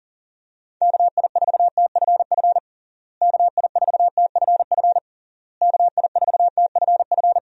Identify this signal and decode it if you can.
Morse code, heard as a keyed tone, about 40 wpm, 700 Hz. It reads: KI4TFF KI4TFF KI4TFF
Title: USA Call Signs - 6 Characters - Repeated 3x - 40wpm